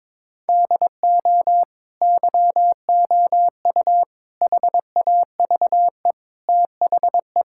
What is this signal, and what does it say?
Morse code, heard as a keyed tone, about 22 wpm, 700 Hz. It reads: DO YOU HAVE THE
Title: Sentences from Top 1000 Words - Review Part 5 - 22wpm